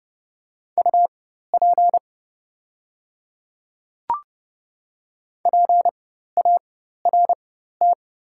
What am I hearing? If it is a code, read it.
Morse code, heard as a keyed tone, about 30 wpm, spaced as for 14 wpm, 690 Hz. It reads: UP PART
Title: 4x Character Spacing - Top 200 Words - 30wpm